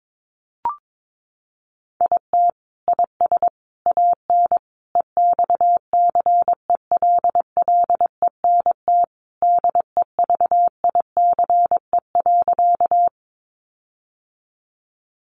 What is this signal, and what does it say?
Morse code, heard as a keyed tone, about 22 wpm, 700 Hz. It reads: IT IS AN EXCELLENT DEVICE.
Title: Sentences from Top 1500 Words 22wpm